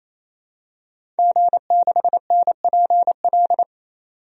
Morse code, heard as a keyed tone, about 28 wpm, 700 Hz. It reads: G6NPL
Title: DXCC-100 Call Signs - 5 Characters - 28wpm